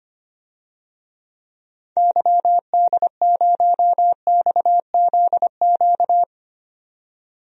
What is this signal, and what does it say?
Morse code, heard as a keyed tone, about 25 wpm, 700 Hz. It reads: YD0XZQ
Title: DXCC-100 Call Signs - 6 Characters - 25wpm